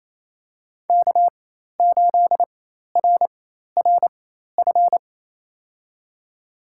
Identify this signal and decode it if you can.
Morse code, heard as a keyed tone, about 28 wpm, spaced as for 13 wpm, 700 Hz. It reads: K8RRF
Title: USA Call Signs - 5 Characters - 4x Character Spacing - 28wpm